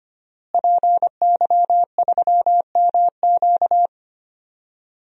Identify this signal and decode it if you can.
Morse code, heard as a keyed tone, about 25 wpm, 700 Hz. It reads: PY3MQ
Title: DXCC-100 Call Signs - 5 Characters - Repeated 2x - 25wpm